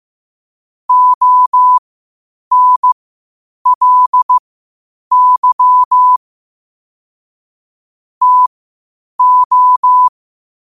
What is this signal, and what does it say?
Morse code, heard as a keyed tone, about 15 wpm, spaced as for 8 wpm, 1000 Hz. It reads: ONLY TO